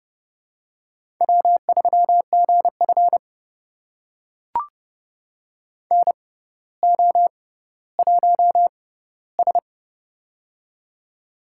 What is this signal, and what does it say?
Morse code, heard as a keyed tone, about 30 wpm, spaced as for 15 wpm, 700 Hz. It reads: W3GF NO1S
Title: USA Call Signs - 6x Character Spacing - 30wpm